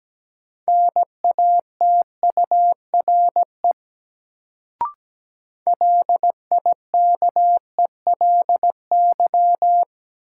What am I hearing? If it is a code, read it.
Morse code, heard as a keyed tone, about 17 wpm, 700 Hz. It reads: NATURE LIKELY
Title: Sentences from Top 600 Words - New Word Drill - 17wpm